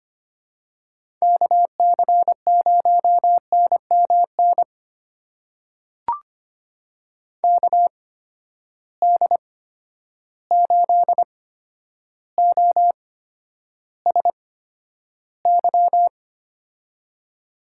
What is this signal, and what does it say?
Morse code, heard as a keyed tone, about 25 wpm, spaced as for 11 wpm, 700 Hz. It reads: KC0NMN KD8OSY
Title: USA Call Signs - 6 Characters - 8x Character Spacing - 25wpm